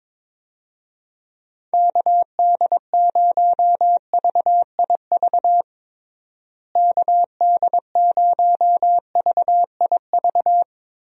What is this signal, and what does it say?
Morse code, heard as a keyed tone, about 22 wpm, 700 Hz. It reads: KD0VIV KD0VIV
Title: USA Call Signs - 6 Characters - Repeated 3x - 22wpm